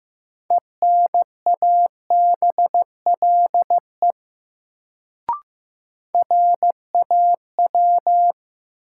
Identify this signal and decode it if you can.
Morse code, heard as a keyed tone, about 15 wpm, 700 Hz. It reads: ENABLE RAW
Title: Sentences from Top 1500 Words - New Word Drill Alt - 15wpm